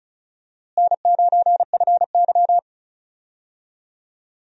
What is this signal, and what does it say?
Morse code, heard as a keyed tone, about 34 wpm, 700 Hz. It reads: N9FY
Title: USA Call Signs - 4 Characters - Repeated 2x - 35wpm